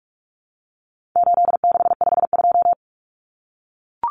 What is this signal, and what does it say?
Morse code, heard as a keyed tone, about 44 wpm, 700 Hz. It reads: 8652
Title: Common Cut Numbers - 4-Digit 45wpm